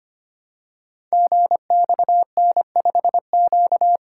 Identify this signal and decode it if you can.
Morse code, heard as a keyed tone, about 25 wpm, 700 Hz. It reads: GXN5Q
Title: Code Groups - 5 Letter:Numbers - 25wpm